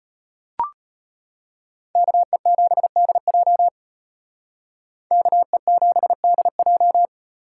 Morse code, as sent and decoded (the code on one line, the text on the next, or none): -.- . --... -.. .--- / -.- . --... -.. .---
KE7DJ KE7DJ